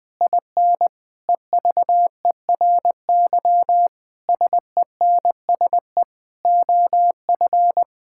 Morse code, heard as a keyed tone, about 20 wpm, 700 Hz. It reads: IN EVERY SENSE OF
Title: Sentences with 6-Word Phrases - Spoken Beginning - 20wpm